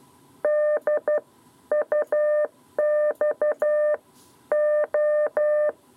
Morse code, heard as a keyed tone, about 12 wpm, 600 Hz. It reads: D UX O
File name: 7b lezione cw 2